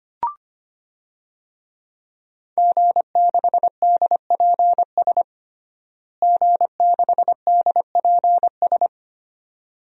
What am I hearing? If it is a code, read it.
Morse code, heard as a keyed tone, about 25 wpm, 700 Hz. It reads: G6DPS G6DPS
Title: DXCC-100 Call Signs - 5 Characters - Repeated 2x - 25wpm